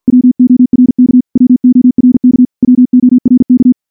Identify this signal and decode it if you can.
Morse code, heard as a keyed tone, about 50 wpm, 250 Hz. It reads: WORK WORK WORK
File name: morse / morse-training-50WPM